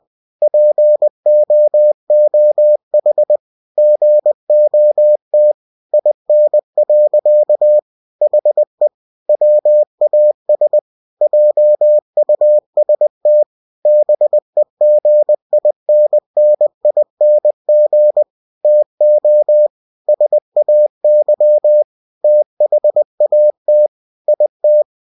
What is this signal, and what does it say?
Morse code, heard as a keyed tone, about 20 wpm, 600 Hz. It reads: POOH GOT IN. HE WAS JUST BEGINNING TO SAY THAT IT
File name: cw-books / winnie-the-pooh / winnie-20wpm-page-0100